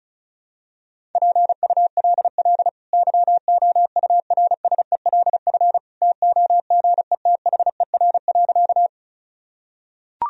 Morse code, heard as a keyed tone, about 36 wpm, 700 Hz. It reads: PULL YOURSELF TOGETHER.
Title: Sentences from Top 1000 Words - Review Part 2 - 35wpm